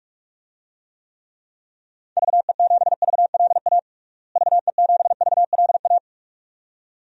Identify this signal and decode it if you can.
Morse code, heard as a keyed tone, about 42 wpm, 700 Hz. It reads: VE7VLA VE7VLA
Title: DXCC-100 Call Signs - 6 Characters - Repeated 2x - 45wpm